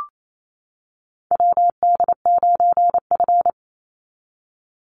Morse code, heard as a keyed tone, about 28 wpm, 700 Hz. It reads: WD9F
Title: NA SOTA Chaser Callsigns 28wpm